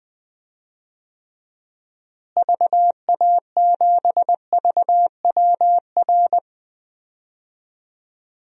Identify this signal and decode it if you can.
Morse code, heard as a keyed tone, about 20 wpm, 700 Hz. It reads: VA7VWR
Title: DXCC-100 Call Signs - 6 Characters - Repeated 2x - 20wpm